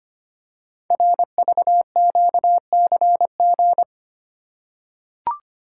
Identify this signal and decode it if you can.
Morse code, heard as a keyed tone, about 25 wpm, 700 Hz. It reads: RVQCG
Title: Code Groups - 5 Letters - 25wpm